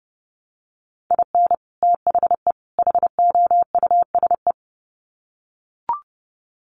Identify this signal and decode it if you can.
Morse code, heard as a keyed tone, about 30 wpm, 700 Hz. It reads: IN THE HOUSE
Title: Sets of 3 Words 30wpm